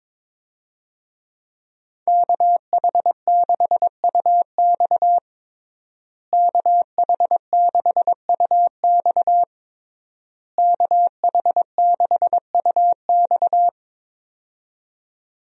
Morse code, heard as a keyed tone, about 22 wpm, 700 Hz. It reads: KH6UX KH6UX KH6UX
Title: USA Call Signs - 5 Characters - Repeated 3x - 22wpm